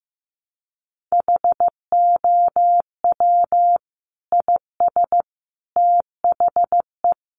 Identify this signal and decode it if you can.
Morse code, heard as a keyed tone, about 15 wpm, 700 Hz. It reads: HOW IS THE
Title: Sentences from Top 800 Words 15wpm